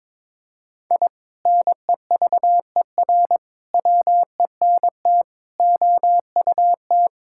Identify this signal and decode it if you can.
Morse code, heard as a keyed tone, about 22 wpm, 700 Hz. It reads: I NEVER WENT OUT
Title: Sentences from Top 1000 Words - Review Part 5 - 22wpm